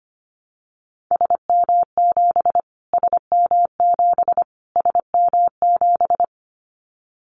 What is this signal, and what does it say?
Morse code, heard as a keyed tone, about 25 wpm, 700 Hz. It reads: SM7 SM7 SM7
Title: Callsign Prefixes for Top 25 DXCC - Repeated 3x - 25wpm